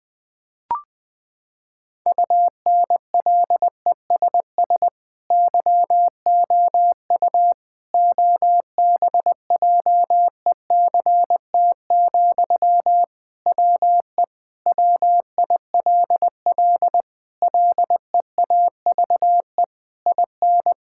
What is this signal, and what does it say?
Morse code, heard as a keyed tone, about 20 wpm, 700 Hz. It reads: UNLESS YOU OBJECT, WE WILL LEAVE IN
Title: Sentences from Top 1000 Words - Review Part 6 - 20wpm